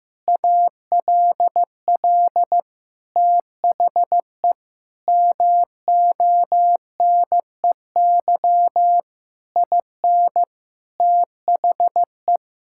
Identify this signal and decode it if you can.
Morse code, heard as a keyed tone, about 15 wpm, 700 Hz. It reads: ALL THE MONEY IN THE
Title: Sentences from Top 1000 Words - Review Part 6 - 15wpm